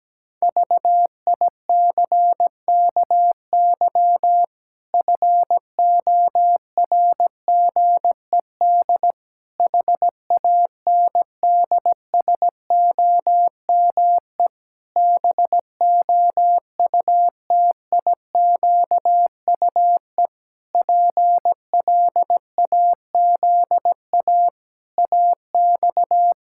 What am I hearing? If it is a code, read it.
Morse code, heard as a keyed tone, about 17 wpm, 700 Hz. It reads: VICKY FORGED HANDSOME BOUTIQUE PLAZA AX
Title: Pangram Challenge v2 - 17wpm